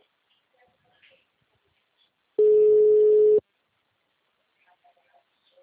Ringback tone (Europe, 425 Hz).
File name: out- -2011- - - .8395